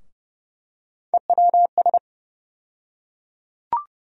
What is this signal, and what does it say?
Morse code, heard as a keyed tone, about 30 wpm, 700 Hz. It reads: EWS
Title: USA Call Sign Postfixes - 3 Letters 30wpm